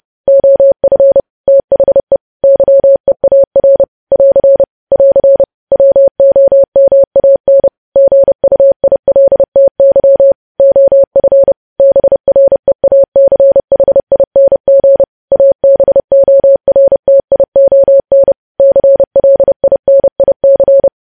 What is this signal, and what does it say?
Morse code, heard as a keyed tone, about 30 wpm, 550 Hz. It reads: OF THE YEAR <AR> <AR> WOMAN GUILTY OF BREACHING ABORTION CLINIC
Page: News Headlines in Morse Code at 30 WPM